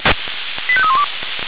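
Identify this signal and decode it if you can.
Five-tone selective call (ZVEI), heard as quick runs of five tones, tones of about 70 ms. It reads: 85312
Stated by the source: ZVEI five-tone selective call received on an SDR.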